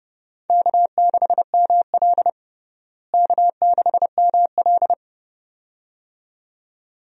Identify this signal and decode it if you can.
Morse code, heard as a keyed tone, about 31 wpm, 700 Hz. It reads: K6ML K6ML
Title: USA Call Signs - 4 Characters - Repeated 2x - 30wpm